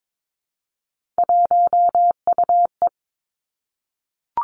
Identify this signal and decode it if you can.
Morse code, heard as a keyed tone, about 22 wpm, 700 Hz. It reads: 1UE
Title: Three Challenging Letter-Number Combinations 22wpm